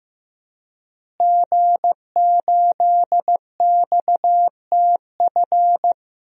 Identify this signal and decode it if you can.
Morse code, heard as a keyed tone, about 15 wpm, 700 Hz. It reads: G8XTF
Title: DXCC-100 Call Signs - 5 Characters - Repeated 3x - 15wpm